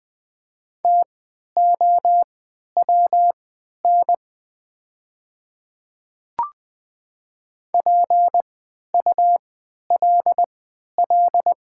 Morse code, heard as a keyed tone, about 20 wpm, spaced as for 11 wpm, 700 Hz. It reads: TOWN PULL
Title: 3x Character Spacing - Top 500 Words - 20wpm